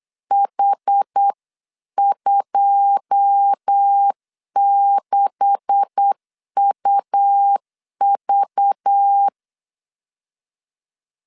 Morse code, heard as a keyed tone, about 8 wpm, 800 Hz. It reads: H26UV
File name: LEZIONI / 9 286 8wpm 3425